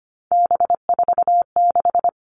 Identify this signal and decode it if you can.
Morse code, heard as a keyed tone, about 25 wpm, 700 Hz. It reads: B46
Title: All Cut Numbers - 3-Digit 25wpm